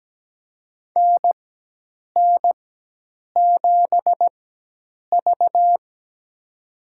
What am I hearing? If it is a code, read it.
Morse code, heard as a keyed tone, about 17 wpm, spaced as for 8 wpm, 700 Hz. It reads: NN7V
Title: USA Call Signs - 4x Character Spacing - 17wpm